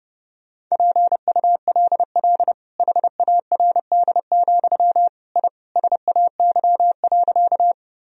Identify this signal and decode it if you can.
Morse code, heard as a keyed tone, about 32 wpm, 700 Hz. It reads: PULL HARD, I SAY.
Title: Sentences from Top 1000 Words - Review Part 2 - 30wpm